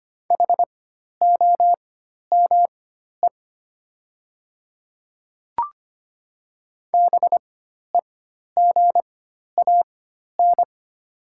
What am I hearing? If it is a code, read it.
Morse code, heard as a keyed tone, about 25 wpm, spaced as for 12 wpm, 700 Hz. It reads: HOME BEGAN
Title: 4x Character Spacing - Top 300 Words - 25wpm